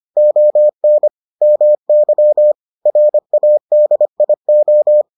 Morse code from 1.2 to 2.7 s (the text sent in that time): MY